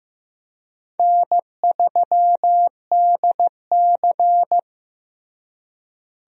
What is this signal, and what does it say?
Morse code, heard as a keyed tone, about 15 wpm, 700 Hz. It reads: N3DC
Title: USA Call Signs - 4 Characters - Repeated 2x - 15wpm